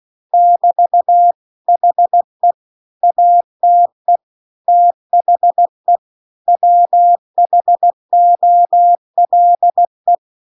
Morse code, heard as a keyed tone, about 16 wpm, 700 Hz. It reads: <BT> HE ATE THE WHOLE